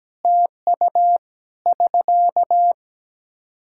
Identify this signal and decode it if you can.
Morse code, heard as a keyed tone, about 17 wpm, 700 Hz. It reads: TU <SK>
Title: Practice QSO - Exchange 4 - Ending - 17wpm